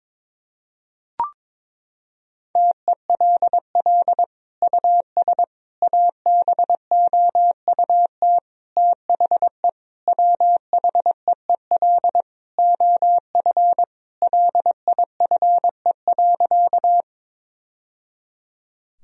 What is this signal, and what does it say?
Morse code, heard as a keyed tone, about 22 wpm, 700 Hz. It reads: TELL US ABOUT THE WHEEL OF LIFE.
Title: Sentences from Top 1000 Words - Review Part 3 - 22wpm